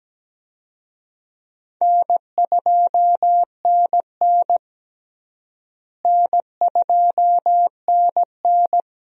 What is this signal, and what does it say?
Morse code, heard as a keyed tone, about 17 wpm, 700 Hz. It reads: N2NN N2NN
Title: USA Call Signs - 4 Characters - Repeated 3x - 17wpm